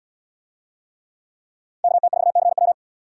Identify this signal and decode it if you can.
Morse code, heard as a keyed tone, about 73 wpm, 700 Hz. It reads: KT4CW